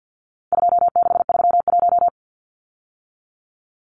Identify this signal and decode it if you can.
Morse code, heard as a keyed tone, about 50 wpm, 700 Hz. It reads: 2631
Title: Numbers - 4-Digit 50wpm